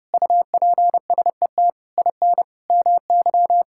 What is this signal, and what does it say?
Morse code, heard as a keyed tone, about 30 wpm, 700 Hz. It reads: UPSET IN MY
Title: Sentences from Top 1000 Words - Review Part 5 - 30wpm